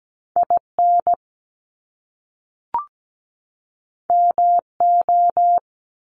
Morse code, heard as a keyed tone, about 17 wpm, 700 Hz. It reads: IN MO
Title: US States 17wpm